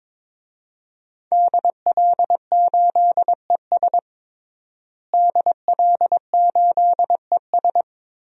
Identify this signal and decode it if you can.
Morse code, heard as a keyed tone, about 22 wpm, 700 Hz. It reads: DL8ES DL8ES
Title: DXCC-100 Call Signs - 5 Characters - Repeated 3x - 22wpm